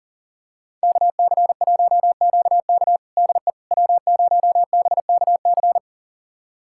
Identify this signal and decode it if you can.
Morse code, heard as a keyed tone, about 39 wpm, 690 Hz. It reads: KC1QK DE W0BKC
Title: USA Callsign DE Callsign - 5 Characters - 40wpm